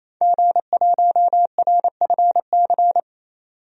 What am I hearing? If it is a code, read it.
Morse code, heard as a keyed tone, about 28 wpm, 700 Hz. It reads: G1RFC
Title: DXCC-100 Call Signs - 5 Characters - 28wpm